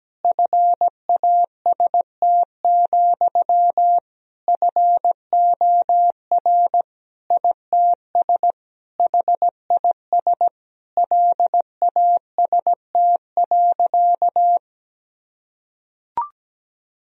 Morse code, heard as a keyed tone, about 17 wpm, 700 Hz. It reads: FAST, FOR ITS HIS LAST.
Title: Sentences from Top 1000 Words - Review Part 2 - 17wpm